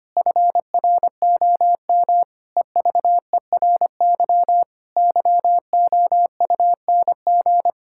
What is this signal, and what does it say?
Morse code, heard as a keyed tone, about 25 wpm, 700 Hz. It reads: FROM EVERY YOUNG
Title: Pangram Challenge v2 - 25wpm